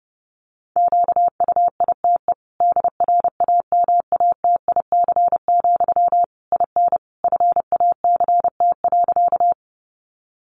Morse code, heard as a keyed tone, about 30 wpm, 700 Hz. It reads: QUITE DRAMATIC, IN FACT.
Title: Sentences from Top 1300 Words 30wpm